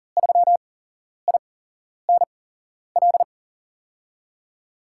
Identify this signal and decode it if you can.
Morse code, heard as a keyed tone, about 40 wpm, spaced as for 11 wpm, 700 Hz. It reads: 3INL
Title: USA Call Signs - 6 Characters - 8x Character Spacing - 40wpm